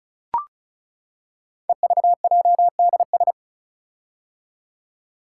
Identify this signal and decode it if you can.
Morse code, heard as a keyed tone, about 36 wpm, 700 Hz. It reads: EVJDS
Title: Code Groups - 5 Letters - 35wpm